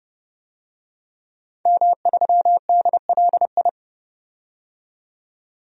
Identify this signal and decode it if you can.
Morse code, heard as a keyed tone, about 29 wpm, 700 Hz. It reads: M3DLI
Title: DXCC-100 Call Signs - 5 Characters - 30wpm